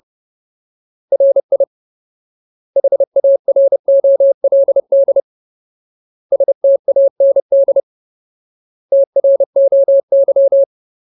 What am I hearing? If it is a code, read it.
Morse code, heard as a keyed tone, about 30 wpm, 550 Hz. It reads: RI HAROLD STAND TROY
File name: Ultimate Word List / 30WPM Ultimate